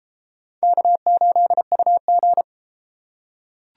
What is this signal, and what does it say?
Morse code, heard as a keyed tone, about 32 wpm, 700 Hz. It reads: K8UG